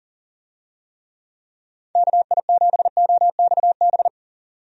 Morse code, heard as a keyed tone, about 40 wpm, 700 Hz. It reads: KI7OXB
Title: USA Call Signs - 6 Characters - 40wpm